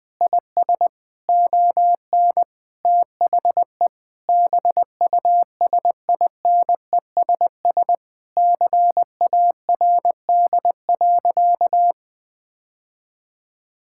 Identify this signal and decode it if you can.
Morse code, heard as a keyed tone, about 21 wpm, 700 Hz. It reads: IS ON THE BUSINESS CARD.
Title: Sentences from Top 1000 Words - Review Part 6 - 20wpm